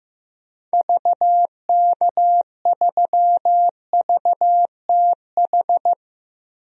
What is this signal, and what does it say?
Morse code, heard as a keyed tone, about 15 wpm, 700 Hz. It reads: VK3VTH
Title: DXCC-100 Call Signs - 6 Characters - 15wpm